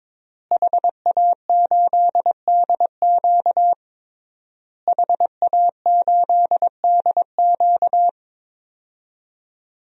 Morse code, heard as a keyed tone, about 22 wpm, 700 Hz. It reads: HA8DQ HA8DQ
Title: DXCC-100 Call Signs - 5 Characters - Repeated 3x - 22wpm